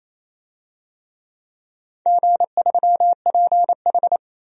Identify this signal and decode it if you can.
Morse code, heard as a keyed tone, about 27 wpm, 690 Hz. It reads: G3PH
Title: DXCC-100 Call Signs - 4 Characters - 28wpm